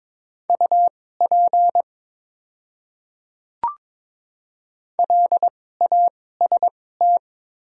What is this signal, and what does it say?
Morse code, heard as a keyed tone, about 22 wpm, spaced as for 16 wpm, 700 Hz. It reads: UP LAST
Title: 2x Character Spacing - Top 500 Words - 22wpm